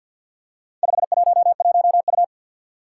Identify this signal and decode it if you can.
Morse code, heard as a keyed tone, about 50 wpm, 700 Hz. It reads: 511U
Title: Four Challenging Letter-Number Combinations 50wpm